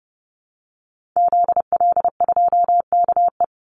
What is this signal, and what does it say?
Morse code, heard as a keyed tone, about 30 wpm, 700 Hz. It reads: ZL2KE
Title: Call Signs Repeated 3x - 30wpm